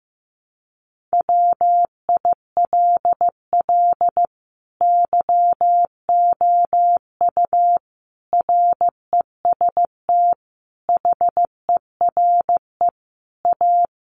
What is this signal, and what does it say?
Morse code, heard as a keyed tone, about 15 wpm, 700 Hz. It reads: WILL YOU REST HERE A
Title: Sentences from Top 500 Words 15wpm